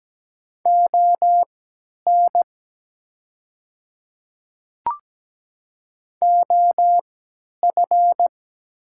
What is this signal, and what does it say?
Morse code, heard as a keyed tone, about 17 wpm, spaced as for 10 wpm, 700 Hz. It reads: ON OF
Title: 3x Character Spacing - Top 200 Words - 17wpm